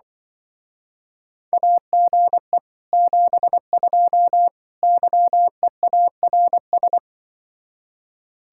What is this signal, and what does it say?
Morse code, heard as a keyed tone, about 25 wpm, 700 Hz. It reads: AGE 72 YEARS